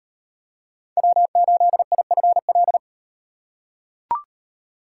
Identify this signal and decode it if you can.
Morse code, heard as a keyed tone, about 38 wpm, 700 Hz. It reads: W8IFL